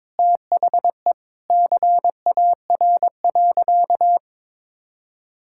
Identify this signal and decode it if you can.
Morse code, heard as a keyed tone, about 21 wpm, 700 Hz. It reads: THE CAR.
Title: Sentences from Top 1000 Words - Review Part 5 - 22wpm